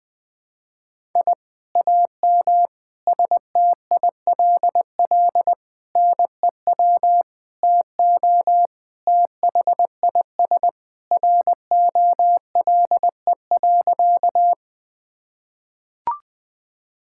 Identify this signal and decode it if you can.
Morse code, heard as a keyed tone, about 21 wpm, 700 Hz. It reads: I AM STILL NEW TO THIS ROLE.
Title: Sentences from Top 1000 Words - Review Part 6 - 20wpm